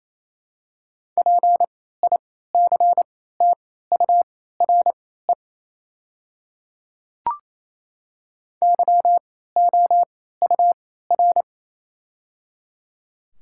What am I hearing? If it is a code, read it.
Morse code, heard as a keyed tone, about 28 wpm, spaced as for 16 wpm, 700 Hz. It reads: PICTURE YOUR
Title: 3x Character Spacing - Top 300 Words - 28wpm